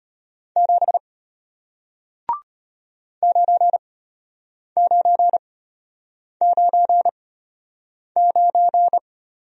Morse code, heard as a keyed tone, about 34 wpm, spaced as for 8 wpm, 700 Hz. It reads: 7 9999